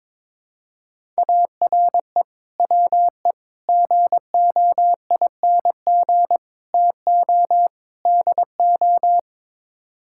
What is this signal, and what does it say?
Morse code, heard as a keyed tone, about 22 wpm, 700 Hz. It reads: ARE WE GOING TO DO
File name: practice / 22wpm / Build A Phrase - 5-Word Phrases - 22wpm